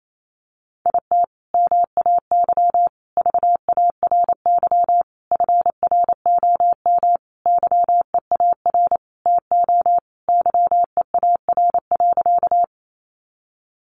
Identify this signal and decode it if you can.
Morse code, heard as a keyed tone, about 28 wpm, 700 Hz. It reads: IT MAY VARY FROM YEAR TO YEAR.
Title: Sentences from Top 1200 Words 28wpm